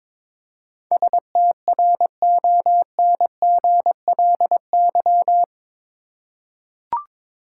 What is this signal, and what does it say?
Morse code, heard as a keyed tone, about 21 wpm, 700 Hz. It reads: STRONGLY
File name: practice / 22wpm / Eight Letter Words - 22wpm